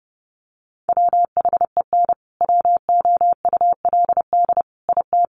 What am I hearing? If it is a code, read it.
Morse code, heard as a keyed tone, about 30 wpm, 700 Hz. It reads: WHEN WOULD IT
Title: Sentences from Top 1200 Words 30wpm